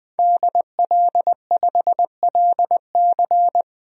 Morse code, heard as a keyed tone, about 20 wpm, 700 Hz. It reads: DL5LC
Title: DXCC-100 Call Signs - 5 Characters - 20wpm